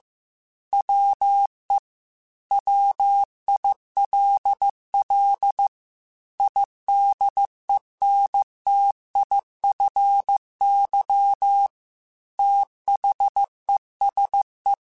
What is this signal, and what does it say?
Morse code, heard as a keyed tone, about 15 wpm, 800 Hz. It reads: WE WILL IDENTIFY THESE
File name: Technical Subjects / Technical Subjects-mp3s / Sw Pwr Sup @13